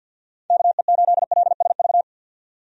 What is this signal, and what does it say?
Morse code, heard as a keyed tone, about 52 wpm, 700 Hz. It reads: KE8LSV
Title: USA Call Signs - 6 Characters - 50wpm